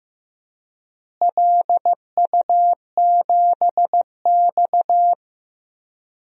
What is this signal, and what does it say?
Morse code, heard as a keyed tone, about 15 wpm, 700 Hz. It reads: LU7X
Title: DXCC-100 Call Signs - 4 Characters - Repeated 3x - 15wpm